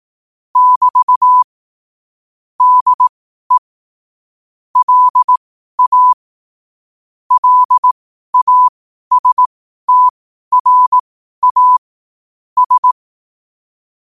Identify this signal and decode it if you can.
Morse code, heard as a keyed tone, about 18 wpm, spaced as for 12 wpm, 1000 Hz. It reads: <BT> DE LA LASTRA S
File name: qotdfiles / 12wpm